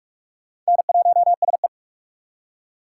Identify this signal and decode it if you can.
Morse code, heard as a keyed tone, about 59 wpm, 700 Hz. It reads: N1SE